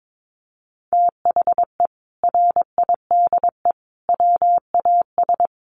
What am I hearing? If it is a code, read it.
Morse code, heard as a keyed tone, about 22 wpm, 700 Hz. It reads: THE RIDE WAS
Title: Sentences from Top 1200 Words 22wpm